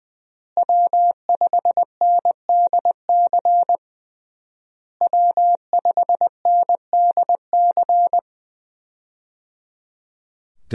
Morse code, heard as a keyed tone, about 20 wpm, 700 Hz. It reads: W5NDC W5NDC
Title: USA Call Signs - 5 Characters - Repeated 3x - 20wpm